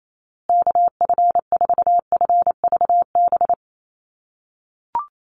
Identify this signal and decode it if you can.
Morse code, heard as a keyed tone, about 28 wpm, 700 Hz. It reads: KF4FVB
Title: USA Call Signs Repeated 3x 28wpm